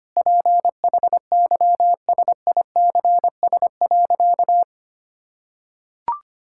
Morse code, heard as a keyed tone, about 25 wpm, 700 Hz. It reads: PHYSICS.
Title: Sentences from Top 1000 Words - Review Part 5 - 25wpm